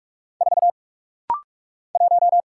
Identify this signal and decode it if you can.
Morse code, heard as a keyed tone, about 46 wpm, 700 Hz. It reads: 4 1
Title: Single Letter-Number - Rapid-Fire 45wpm